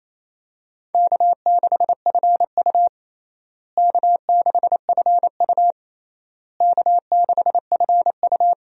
Morse code, heard as a keyed tone, about 28 wpm, 700 Hz. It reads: K6FU K6FU K6FU
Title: USA Call Signs - 4 Characters - Repeated 3x - 28wpm